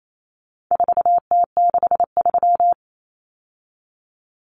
Morse code, heard as a keyed tone, about 28 wpm, 700 Hz. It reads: 4T63
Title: Common Cut Numbers - 4-Digit 28wpm